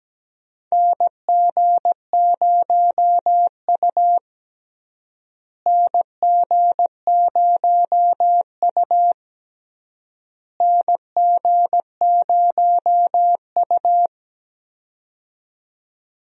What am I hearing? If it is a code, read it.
Morse code, heard as a keyed tone, about 17 wpm, 700 Hz. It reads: NG0U NG0U NG0U
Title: USA Call Signs - 4 Characters - Repeated 3x - 17wpm